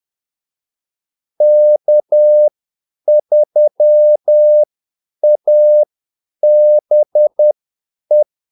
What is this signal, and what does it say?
Morse code, heard as a keyed tone, about 10 wpm, spaced as for 8 wpm, 600 Hz. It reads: K3ABE
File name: Callsigns-10-8wpm